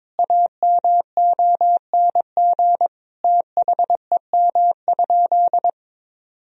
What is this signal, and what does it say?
Morse code, heard as a keyed tone, about 22 wpm, 700 Hz. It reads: AMONG THEM?
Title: Sentences from Top 1000 Words - Review Part 4 - 22wpm